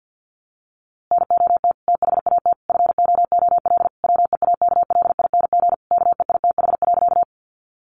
Morse code, heard as a keyed tone, about 48 wpm, 700 Hz. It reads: NOT THAT FOOL WEAKLING KEITH.
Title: A Sound Of Thunder 50wpm